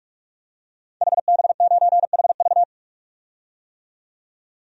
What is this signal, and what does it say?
Morse code, heard as a keyed tone, about 46 wpm, 700 Hz. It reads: HB9HV
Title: DXCC-100 Call Signs - 5 Characters - 45wpm